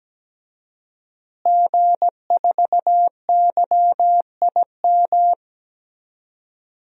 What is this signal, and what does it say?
Morse code, heard as a keyed tone, about 17 wpm, 700 Hz. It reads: G4YIM
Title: DXCC-100 Call Signs - 5 Characters - Repeated 3x - 17wpm